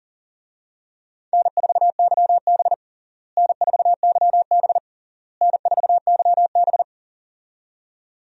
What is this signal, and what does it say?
Morse code, heard as a keyed tone, about 40 wpm, 700 Hz. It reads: N4YB N4YB N4YB
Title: USA Call Signs - 4 Characters - Repeated 3x - 40wpm